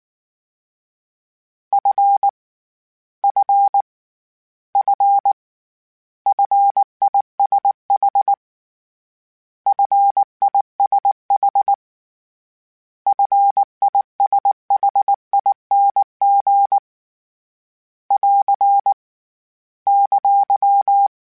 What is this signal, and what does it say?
Morse code, heard as a keyed tone, about 19 wpm, 800 Hz. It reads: F F F FISH FISH FISHING <AR> !